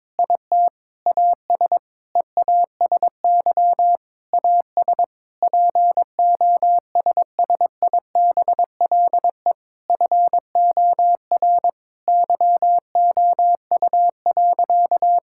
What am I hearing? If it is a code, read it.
Morse code, heard as a keyed tone, about 22 wpm, 700 Hz. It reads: IT AS EASY AS POSSIBLE FOR YOU.
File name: Sentences with 7-Word Phrases - 22wpm